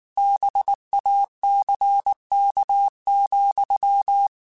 Morse code, heard as a keyed tone, about 19 wpm, 800 Hz. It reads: BACK,
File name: QSOs / QSOs-mp3s / kc7sac@18